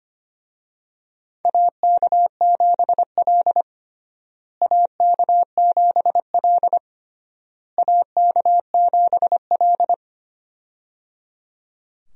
Morse code, heard as a keyed tone, about 25 wpm, 700 Hz. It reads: AK7L AK7L AK7L
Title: USA Call Signs - 4 Characters - Repeated 3x - 25wpm